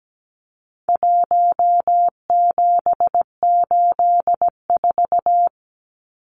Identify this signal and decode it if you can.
Morse code, heard as a keyed tone, about 17 wpm, 700 Hz. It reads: 1784
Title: Numbers - 4-Digit 17wpm